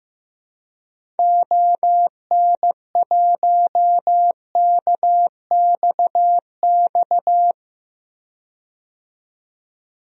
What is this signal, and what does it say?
Morse code, heard as a keyed tone, about 15 wpm, 700 Hz. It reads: ON1KXX
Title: DXCC-100 Call Signs - 6 Characters - 15wpm